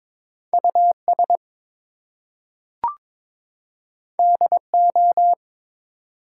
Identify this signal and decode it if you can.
Morse code, heard as a keyed tone, about 22 wpm, 700 Hz. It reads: US DO
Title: Two Letter Words - 22wpm